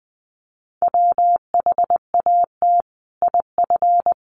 Morse code, heard as a keyed tone, about 20 wpm, 700 Hz. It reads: WHAT IF
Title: Sentences from Top 1400 Words 20wpm